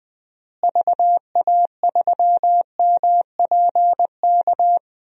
Morse code, heard as a keyed tone, about 20 wpm, 700 Hz. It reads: VA3MPK
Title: DXCC-100 Call Signs - 6 Characters - Repeated 2x - 20wpm